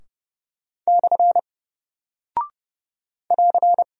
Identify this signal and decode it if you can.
Morse code, heard as a keyed tone, about 30 wpm, 700 Hz. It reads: / <AR>
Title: Single Character (Letter:Number:Punctuation:Prosign) Alt 30wpm